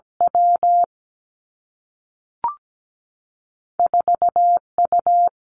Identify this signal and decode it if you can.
Morse code, heard as a keyed tone, about 17 wpm, 700 Hz. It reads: W 4 U
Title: Two Challenging Letter-Number Combinations 17wpm